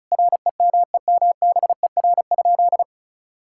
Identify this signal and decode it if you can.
Morse code, heard as a keyed tone, about 36 wpm, 700 Hz. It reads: REMEMBER?
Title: Sentences from Top 1000 Words - Review Part 6 - 35wpm